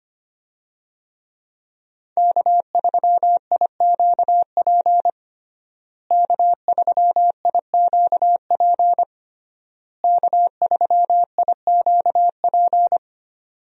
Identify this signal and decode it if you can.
Morse code, heard as a keyed tone, about 25 wpm, 700 Hz. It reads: K3IQP K3IQP K3IQP
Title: USA Call Signs - 5 Characters - Repeated 3x - 25wpm